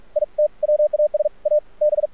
Morse code, heard as a keyed tone, about 46 wpm, 600 Hz. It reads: IT WAS A B
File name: archived / CRSnet